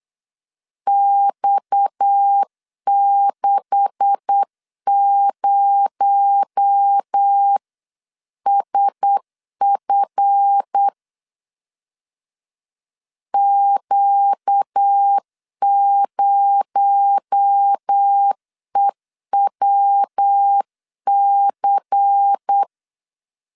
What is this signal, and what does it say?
Morse code, heard as a keyed tone, about 8 wpm, 800 Hz. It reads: X60SF Q0EWC
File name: CW / DET / 1 9 1500C S B 8wpm 5624